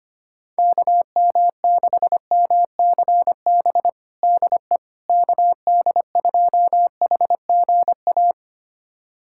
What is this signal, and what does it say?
Morse code, heard as a keyed tone, about 25 wpm, 700 Hz. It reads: KM6MCB DE KD2HGA
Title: USA Callsign DE Callsign 25wpm